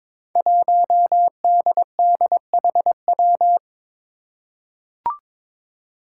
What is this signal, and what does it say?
Morse code, heard as a keyed tone, about 22 wpm, 700 Hz. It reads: 1DDHW
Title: Five Challenging Letter-Number Combinations 22wpm